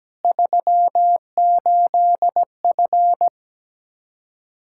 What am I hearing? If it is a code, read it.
Morse code, heard as a keyed tone, about 17 wpm, 700 Hz. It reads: 38F
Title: Practice QSO - Exchange 2 - Info - 17wpm